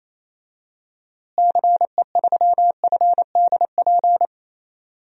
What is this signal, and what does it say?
Morse code, heard as a keyed tone, about 27 wpm, 700 Hz. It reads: CE3FDP
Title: DXCC-100 Call Signs - 6 Characters - Repeated 2x - 28wpm